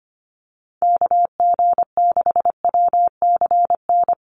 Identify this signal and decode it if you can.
Morse code, heard as a keyed tone, about 25 wpm, 700 Hz. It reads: KG6WCN
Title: USA Call Signs Repeated 3x 25wpm